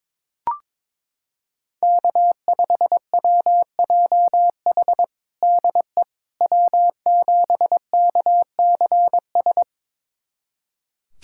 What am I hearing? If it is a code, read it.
Morse code, heard as a keyed tone, about 21 wpm, 700 Hz. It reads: K5WJH DE W7KCS
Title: USA Callsign DE Callsign - 5 Characters - 22wpm